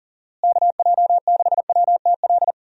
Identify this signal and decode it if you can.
Morse code, heard as a keyed tone, about 39 wpm, 700 Hz. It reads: KJ6WTL
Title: USA Call Signs - 6 Characters - Repeated 3x - 40wpm